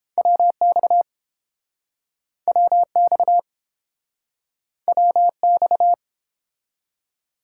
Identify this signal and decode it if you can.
Morse code, heard as a keyed tone, about 30 wpm, 700 Hz. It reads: WX WX WX